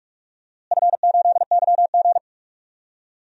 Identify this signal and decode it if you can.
Morse code, heard as a keyed tone, about 43 wpm, 700 Hz. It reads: F8YG
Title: DXCC-100 Call Signs - 4 Characters - Repeated 2x - 45wpm